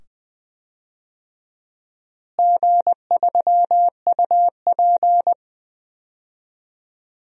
Morse code, heard as a keyed tone, about 20 wpm, 700 Hz. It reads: G3UP
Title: DXCC-100 Call Signs - 4 Characters - Repeated 2x - 20wpm